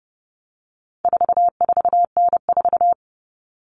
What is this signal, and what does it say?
Morse code, heard as a keyed tone, about 30 wpm, 700 Hz. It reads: 44N4
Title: All Cut Numbers - 4-Digit 30wpm